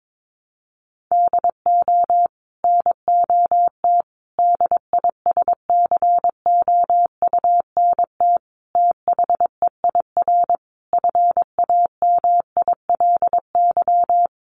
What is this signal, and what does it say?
Morse code, heard as a keyed tone, about 22 wpm, 700 Hz. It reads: DO NOT DISCOUNT THEIR FAMILY
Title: Sentences from Top 1200 Words 22wpm